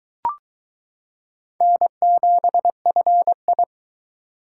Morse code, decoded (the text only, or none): N7FI